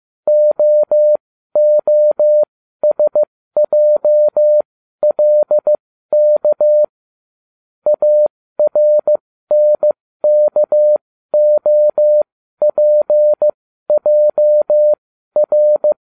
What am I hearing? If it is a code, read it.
Morse code, heard as a keyed tone, about 15 wpm, spaced as for 12 wpm, 600 Hz. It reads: OOSJLK ARNKOPJR